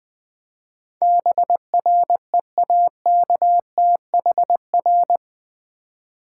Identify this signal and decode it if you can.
Morse code, heard as a keyed tone, about 20 wpm, 700 Hz. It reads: BREAKTHR
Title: Build A Word - 4 Letters At A Time - 12-Letter Words - 20wpm